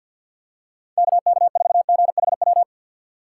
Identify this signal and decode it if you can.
Morse code, heard as a keyed tone, about 52 wpm, 700 Hz. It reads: KK4GHW
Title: USA Call Signs - 6 Characters - 6x Character Spacing - 50wpm